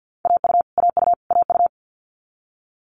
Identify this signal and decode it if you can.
Morse code, heard as a keyed tone, about 50 wpm, 700 Hz. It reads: AU AU AU
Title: Multi-letter Phonemes Repeated 3x 50wpm